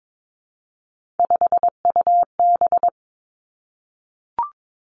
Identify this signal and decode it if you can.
Morse code, heard as a keyed tone, about 22 wpm, 700 Hz. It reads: 5UB
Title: Three Challenging Letter-Number Combinations 22wpm